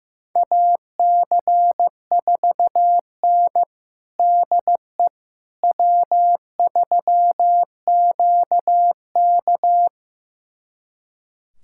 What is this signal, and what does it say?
Morse code, heard as a keyed tone, about 15 wpm, 700 Hz. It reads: AC4N DE W3QK
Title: USA Callsign DE Callsign - 4 Characters - 15wpm